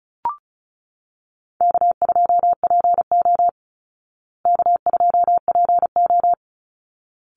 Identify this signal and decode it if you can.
Morse code, heard as a keyed tone, about 34 wpm, 700 Hz. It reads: K2PO K2PO
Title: Easy Call Signs Repeated 2x - 35wpm